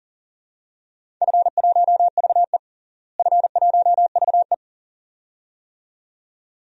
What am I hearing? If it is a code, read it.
Morse code, heard as a keyed tone, about 40 wpm, 700 Hz. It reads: F1VE F1VE
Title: DXCC-100 Call Signs - 4 Characters - Repeated 2x - 40wpm